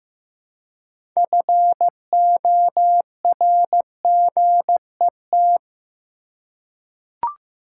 Morse code, heard as a keyed tone, about 15 wpm, 700 Hz. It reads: FORGET
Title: Six Letter Words - 15wpm